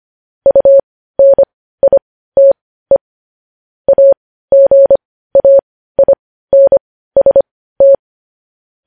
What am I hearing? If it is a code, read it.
Morse code, heard as a keyed tone, about 25 wpm, spaced as for 15 wpm, 550 Hz. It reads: UNITE AGAINST
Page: news headlines in morse code at 15 wpm